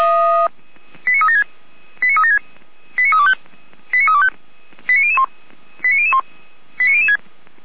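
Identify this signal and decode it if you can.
Five-tone selective call (ZVEI), heard as quick runs of five tones, tones of about 70 ms. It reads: 89376 89376 89325 89325 89001 89001 89006
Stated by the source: ZVEI five-tone selective call received on SDR.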